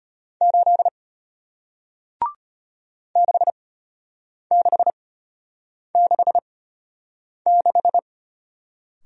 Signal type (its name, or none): Morse code